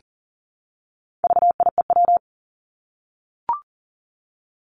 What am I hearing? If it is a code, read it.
Morse code, heard as a keyed tone, about 40 wpm, 700 Hz. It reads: VIEW
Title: Top 751-1000 Words 40wpm